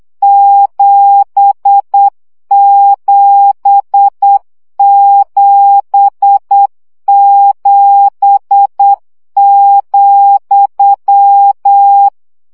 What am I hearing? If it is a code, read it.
Morse code, heard as a keyed tone, about 8 wpm, 800 Hz. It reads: 7777,
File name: CW gyakorlas z 7 vesszo 7 baud